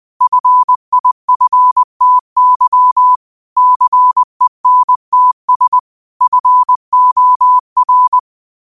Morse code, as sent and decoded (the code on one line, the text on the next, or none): ..-. .. ..-. - -.-- / -.-. -. - ... / ..-. --- .-.
FIFTY CENTS FOR